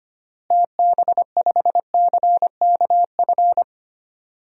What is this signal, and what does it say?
Morse code, heard as a keyed tone, about 25 wpm, 700 Hz. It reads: TB5CKF
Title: DXCC-100 Call Signs - 6 Characters - Repeated 2x - 25wpm